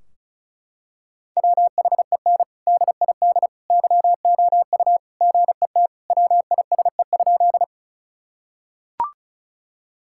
Morse code, heard as a keyed tone, about 36 wpm, 700 Hz. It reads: WHEN DID YOU GET WISE?
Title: Sentences from Top 1600 Words 35wpm